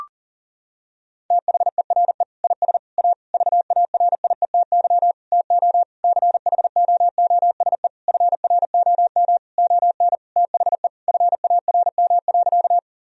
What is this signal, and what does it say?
Morse code, heard as a keyed tone, about 40 wpm, 700 Hz. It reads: THERE IS A VARIETY TO CHOOSE FROM ON THE FARM.
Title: Sentences from Top 1000 Words - Review Part 4 - 40wpm